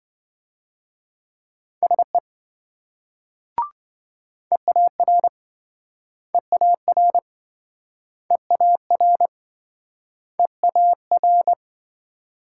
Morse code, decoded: SE EAR EAR EAR EAR

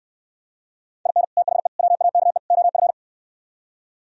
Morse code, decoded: IN THE WAKE OF

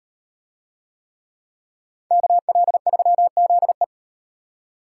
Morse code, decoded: KL3ZE

